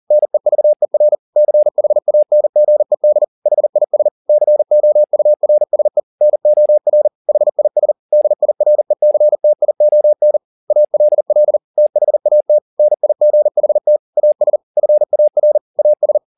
Morse code, heard as a keyed tone, about 40 wpm, 600 Hz. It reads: NEVER CHANGED HIS COURSE NOR HIS DIRECTION ALL THAT NIGHT AS FAR AS